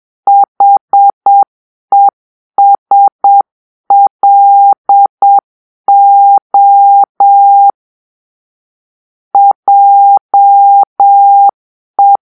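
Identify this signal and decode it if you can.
Morse code, heard as a keyed tone, about 7 wpm, 800 Hz. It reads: HESLO JE
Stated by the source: Morse-code cipher made by a player team for the Kachna puzzle game.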